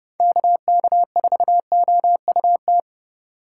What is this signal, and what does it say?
Morse code, heard as a keyed tone, about 40 wpm, spaced as for 35 wpm, 700 Hz. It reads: KK4OUT
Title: Speed Racing - USA 2x3 Call Signs 30wpm